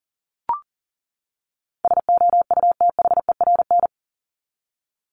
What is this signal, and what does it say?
Morse code, heard as a keyed tone, about 40 wpm, 700 Hz. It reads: SOUTHERN
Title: Top 1751-2000 Words 40wpm